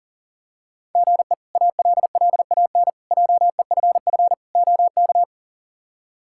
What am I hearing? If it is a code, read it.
Morse code, heard as a keyed tone, about 40 wpm, 690 Hz. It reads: GE ALLAN JEFF OK
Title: K1USN SST QSO Practice - 40wpm